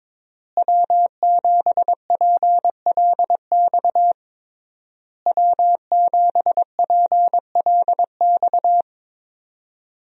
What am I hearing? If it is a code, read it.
Morse code, heard as a keyed tone, about 21 wpm, 700 Hz. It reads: W7PLX W7PLX
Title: USA Call Signs - 5 Characters - Repeated 3x - 22wpm